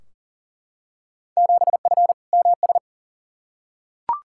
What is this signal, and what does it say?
Morse code, heard as a keyed tone, about 40 wpm, 700 Hz. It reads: 7F MS
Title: Field Day Exchanges Encore - 40wpm